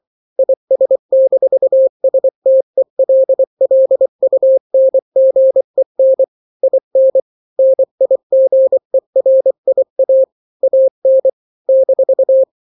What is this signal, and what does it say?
Morse code, heard as a keyed tone, about 24 wpm, 530 Hz. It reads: IS-STELLUNGEN IN NIGERIA AN -
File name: daily / 23wpm